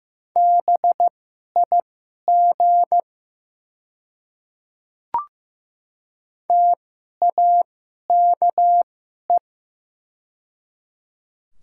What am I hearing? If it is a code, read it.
Morse code, heard as a keyed tone, about 15 wpm, spaced as for 11 wpm, 700 Hz. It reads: BIG TAKE
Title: 2x Character Spacing - Top 300 Words - 15wpm